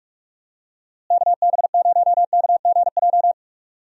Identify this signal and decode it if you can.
Morse code, heard as a keyed tone, about 44 wpm, 700 Hz. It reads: KB0KGJ
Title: USA Call Signs - 6 Characters - Repeated 3x - 45wpm